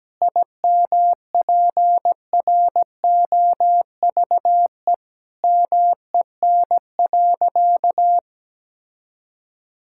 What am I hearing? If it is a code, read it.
Morse code, heard as a keyed tone, about 17 wpm, 700 Hz. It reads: IMPROVE MEN.
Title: Sentences from Top 1000 Words - Review Part 5 - 17wpm